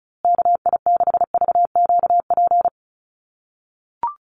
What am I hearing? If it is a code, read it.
Morse code, heard as a keyed tone, about 34 wpm, 700 Hz. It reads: KI6VQP
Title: USA Call Signs Repeated 3x 35wpm